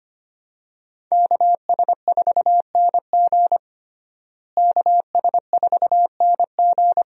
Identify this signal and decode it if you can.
Morse code, heard as a keyed tone, about 25 wpm, 700 Hz. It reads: KS4NG KS4NG
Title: USA Call Signs - 5 Characters - Repeated 2x - 25wpm